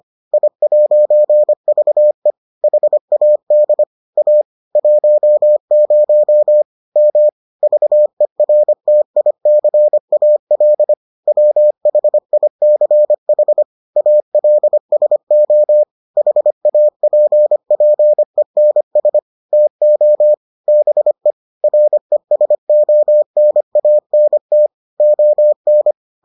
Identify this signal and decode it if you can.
Morse code, heard as a keyed tone, about 25 wpm, 600 Hz. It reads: I'VE HAD A 10 M VERTICAL WHICH ALSO HAPPENS TO BE RESONANT ON